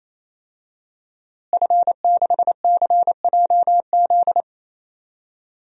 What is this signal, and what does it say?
Morse code, heard as a keyed tone, about 27 wpm, 700 Hz. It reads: F6CJZ
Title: DXCC-100 Call Signs - 5 Characters - Repeated 3x - 28wpm